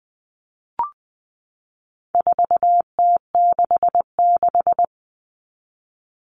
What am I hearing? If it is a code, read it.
Morse code, heard as a keyed tone, about 20 wpm, 700 Hz. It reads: 4T66
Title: All Cut Numbers - 4-Digit 20wpm